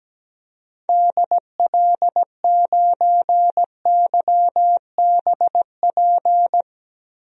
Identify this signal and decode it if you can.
Morse code, heard as a keyed tone, about 17 wpm, 700 Hz. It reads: DL9YBP
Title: DXCC-100 Call Signs - 6 Characters - Repeated 2x - 17wpm